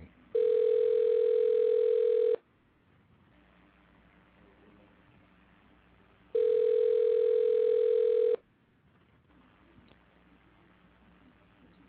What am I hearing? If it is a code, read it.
Ringback tone (North America), 440 + 480 Hz.